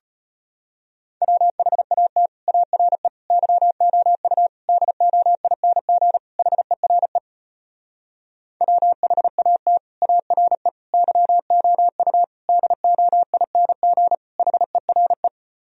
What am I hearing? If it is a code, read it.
Morse code, heard as a keyed tone, about 33 wpm, 700 Hz. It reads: WHAT ARE YOU DOING HERE WHAT ARE YOU DOING HERE